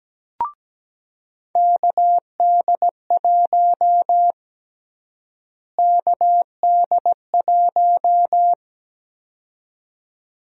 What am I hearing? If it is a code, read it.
Morse code, heard as a keyed tone, about 17 wpm, 700 Hz. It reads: KD1 KD1
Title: USA Call Sign Prefixes - Repeated 2x - 17wpm